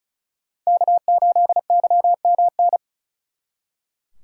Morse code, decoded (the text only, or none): K8YMN